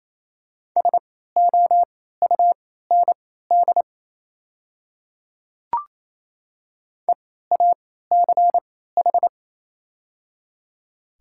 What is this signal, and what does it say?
Morse code, heard as a keyed tone, about 28 wpm, spaced as for 16 wpm, 690 Hz. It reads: SOUND EACH
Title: 3x Character Spacing - Top 100 Words - 28wpm